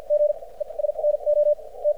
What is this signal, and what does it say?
Morse code, heard as a keyed tone, about 56 wpm, spaced as for 29 wpm, 590 Hz. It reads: GES NOT
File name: archived / CRSnet